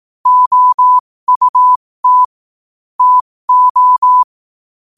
Morse code, heard as a keyed tone, about 18 wpm, spaced as for 15 wpm, 1000 Hz. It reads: OUT TO
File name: qotdfiles / 15wpm